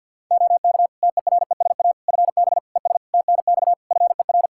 Morse code, heard as a keyed tone, about 52 wpm, 700 Hz. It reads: OK TERESA FB ES TNX FER
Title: Practice QSO - Exchange 2 - Info - 50wpm